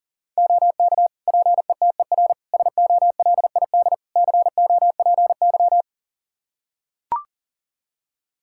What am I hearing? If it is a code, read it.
Morse code, heard as a keyed tone, about 40 wpm, 700 Hz. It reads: OK PETER SOLID COPY
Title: K1USN SST QSO Practice 40wpm